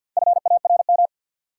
Morse code, heard as a keyed tone, about 53 wpm, 700 Hz. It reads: FARM
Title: Sentences from Top 1000 Words - Review Part 2 - New Word Drill - 50wpm